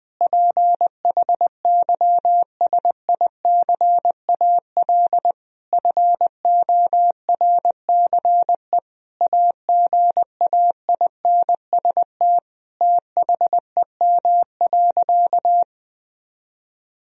Morse code, heard as a keyed tone, about 20 wpm, 700 Hz. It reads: PHYSICAL FORCE AGAINST THEM.
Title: Sentences from Top 1000 Words - Review Part 4 - 20wpm